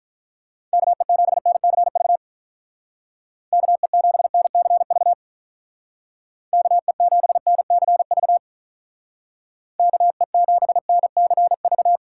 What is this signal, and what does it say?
Morse code, heard as a keyed tone, about 46 wpm, 700 Hz. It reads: KE7NCV KE7NCV KE7NCV KE7NCV